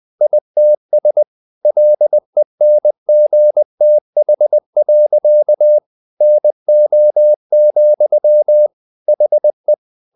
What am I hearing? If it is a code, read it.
Morse code, heard as a keyed tone, about 20 wpm, 600 Hz. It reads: ITS LENGTH. NO, HE
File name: the old man and the sea part 2 20wpm